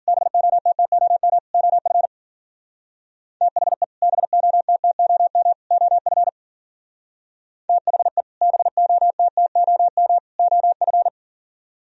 Morse code, heard as a keyed tone, about 49 wpm, 700 Hz. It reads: BOTTOM OF THE BOTTOM OF THE BOTTOM OF
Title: Speed-Racing - Sets of 3 Words 40wpm